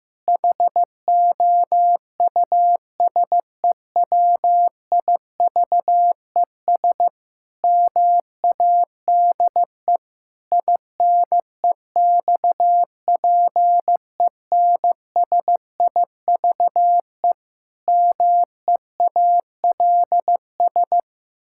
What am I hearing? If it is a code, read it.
Morse code, heard as a keyed tone, about 16 wpm, 700 Hz. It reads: HOUSEWIVES MADE INEXPENSIVE MEALS
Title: Pangram Challenge v2 - 15wpm